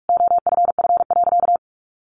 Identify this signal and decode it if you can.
Morse code, heard as a keyed tone, about 44 wpm, 700 Hz. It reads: OFF.
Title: Sentences from Top 1000 Words 45wpm